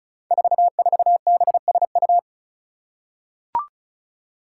Morse code, heard as a keyed tone, about 34 wpm, 700 Hz. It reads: 44BSU
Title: Five Challenging Letter-Number Combinations 35wpm